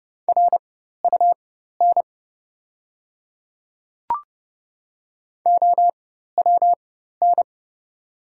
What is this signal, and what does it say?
Morse code, heard as a keyed tone, about 31 wpm, spaced as for 14 wpm, 690 Hz. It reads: RUN OWN
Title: 4x Character Spacing - Top 300 Words - 30wpm